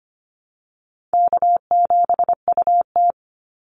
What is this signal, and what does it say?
Morse code, heard as a keyed tone, about 25 wpm, 700 Hz. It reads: K7UT
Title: Easy Call Signs Repeated 3x - 25wpm